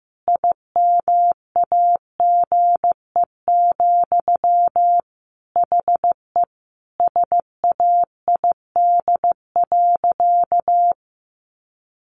Morse code, heard as a keyed tone, about 15 wpm, 700 Hz. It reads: IMAGE, HE SAID.
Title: Sentences from Top 700 Words 15wpm